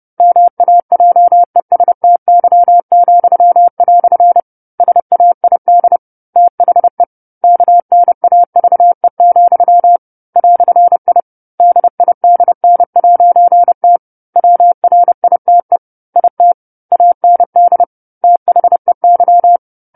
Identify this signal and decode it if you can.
Morse code, heard as a keyed tone, about 30 wpm, 700 Hz. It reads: MAJESTY," SAID THE KNAVE, "I DIDN'T WRITE IT AND THEY